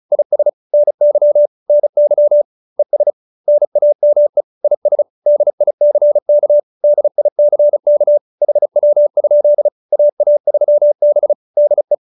Morse code, heard as a keyed tone, about 34 wpm, 590 Hz. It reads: IS NY NY ES NAME IS DICK DICK HW? AA3B DE
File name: QSO302 35